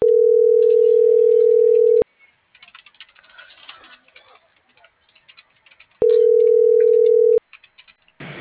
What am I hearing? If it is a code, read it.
Ringback tone (North America), 440 + 480 Hz.